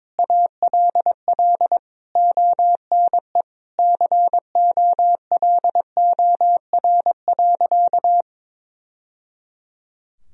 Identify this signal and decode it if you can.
Morse code, heard as a keyed tone, about 22 wpm, 700 Hz. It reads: ALL ONE COLOR.
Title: Sentences from Top 1000 Words - Review Part 2 - 22wpm